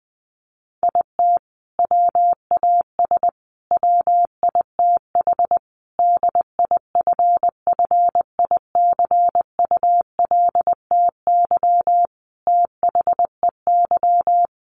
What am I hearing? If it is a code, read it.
Morse code, heard as a keyed tone, about 20 wpm, 700 Hz. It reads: IT WAS WITH DIFFICULTY THEY
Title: Sentences from Top 1500 Words 20wpm